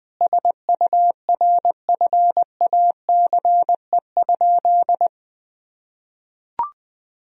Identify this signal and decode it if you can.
Morse code, heard as a keyed tone, about 20 wpm, 700 Hz. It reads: SURFACE?
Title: Sentences from Top 1000 Words - Review Part 3 - 20wpm